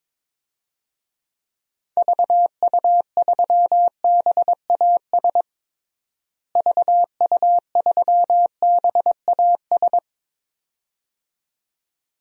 Morse code, heard as a keyed tone, about 22 wpm, 700 Hz. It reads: VU3BAS VU3BAS
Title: DXCC-100 Call Signs - 6 Characters - Repeated 2x - 22wpm